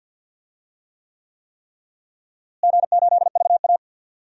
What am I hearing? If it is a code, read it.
Morse code, heard as a keyed tone, about 51 wpm, 700 Hz. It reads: G8VA
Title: DXCC-100 Call Signs - 4 Characters - Repeated 3x - 50wpm